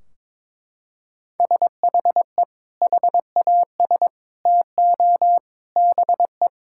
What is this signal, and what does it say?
Morse code, heard as a keyed tone, about 21 wpm, 700 Hz. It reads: SHE HAS TO BE
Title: Sentences from Top 1600 Words 22wpm